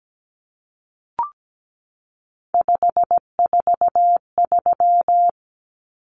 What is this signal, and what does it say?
Morse code, heard as a keyed tone, about 17 wpm, 700 Hz. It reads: 543
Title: Numbers - 3-Digit 17wpm